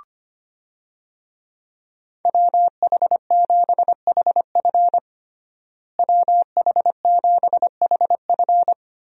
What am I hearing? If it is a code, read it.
Morse code, heard as a keyed tone, about 25 wpm, 700 Hz. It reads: WH7HF WH7HF
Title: USA Call Signs - 5 Characters - Repeated 3x - 25wpm